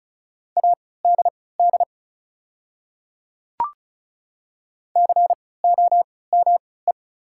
Morse code, heard as a keyed tone, about 36 wpm, spaced as for 20 wpm, 700 Hz. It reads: ADD COME
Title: 3x Character Spacing - Top 200 Words - 35wpm